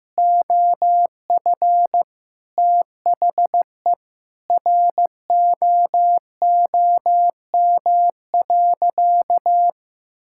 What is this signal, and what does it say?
Morse code, heard as a keyed tone, about 15 wpm, 700 Hz. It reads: OF THE ROOM.
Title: Sentences from Top 1000 Words - Review Part 6 - 15wpm